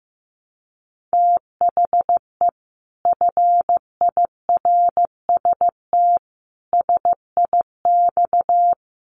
Morse code, heard as a keyed tone, about 15 wpm, 700 Hz. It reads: THE FIRST SIX
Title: Sets of 5 Words Encore2 15wpm